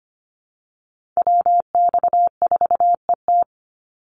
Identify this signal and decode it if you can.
Morse code, heard as a keyed tone, about 25 wpm, 700 Hz. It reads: WX4ET
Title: NA SOTA Chaser Callsigns 25wpm